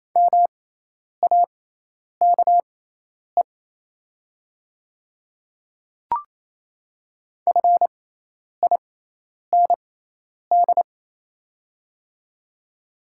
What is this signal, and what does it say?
Morse code, heard as a keyed tone, about 28 wpm, spaced as for 10 wpm, 690 Hz. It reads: MAKE FIND
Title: 6x Character Spacing - Top 100 Words - 28wpm